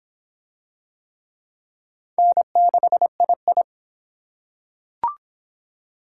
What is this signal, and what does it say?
Morse code, heard as a keyed tone, about 24 wpm, 700 Hz. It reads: N6II